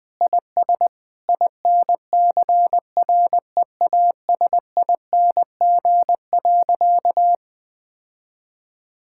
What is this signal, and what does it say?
Morse code, heard as a keyed tone, about 20 wpm, 700 Hz. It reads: IS INCREASING.
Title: Sentences with 2-Word Phrases - Spoken Beginning - 20wpm